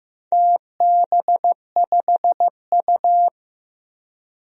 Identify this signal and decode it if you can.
Morse code, heard as a keyed tone, about 16 wpm, 700 Hz. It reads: TB5U
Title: DXCC-100 Call Signs - 4 Characters - 15wpm